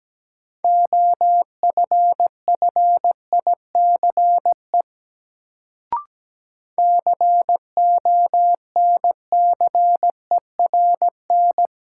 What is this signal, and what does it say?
Morse code, heard as a keyed tone, about 17 wpm, 700 Hz. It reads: OFFICE CONCERN
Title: Sentences from Top 1000 Words - New Word Drill - 17wpm